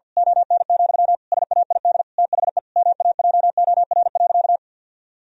Morse code, heard as a keyed tone, about 50 wpm, 700 Hz. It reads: ON, SAID THE MAJOR.
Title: Sentences from Top 1000 Words - Review Part 4 - 50wpm